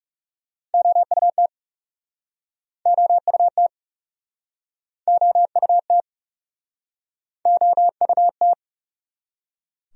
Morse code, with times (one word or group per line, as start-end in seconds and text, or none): 0.7-1.5 OUT
2.9-3.7 OUT
5.1-6.0 OUT
7.5-8.5 OUT